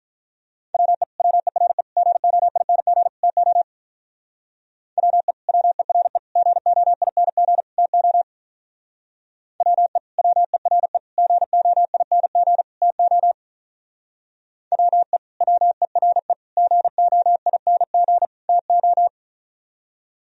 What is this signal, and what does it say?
Morse code, heard as a keyed tone, about 43 wpm, 700 Hz. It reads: WE WERE GOING TO WE WERE GOING TO WE WERE GOING TO WE WERE GOING TO
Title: Speed-Racing - Sets of 4 Words 35wpm